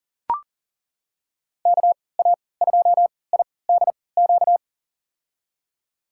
Morse code, heard as a keyed tone, about 40 wpm, spaced as for 23 wpm, 700 Hz. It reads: KA2IDQ
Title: USA Call Signs - 6 Characters - 3x Character Spacing - 40wpm